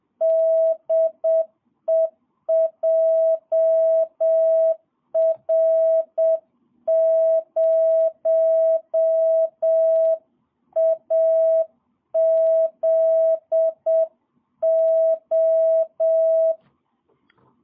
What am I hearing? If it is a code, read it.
Morse code, heard as a keyed tone, about 7 wpm, 650 Hz. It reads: DE JR0AZO
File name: jh8cbh / 2022cw